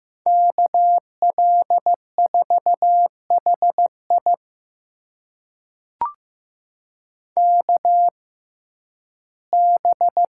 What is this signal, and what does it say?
Morse code, heard as a keyed tone, about 15 wpm, spaced as for 11 wpm, 700 Hz. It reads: KL4HI KB
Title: USA Call Signs - 5 Characters - 6x Character Spacing - 15wpm